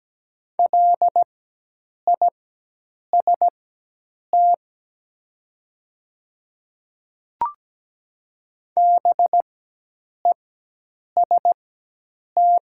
Morse code, heard as a keyed tone, about 17 wpm, spaced as for 8 wpm, 700 Hz. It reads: LIST BEST
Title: 4x Character Spacing - Top 500 Words - 17wpm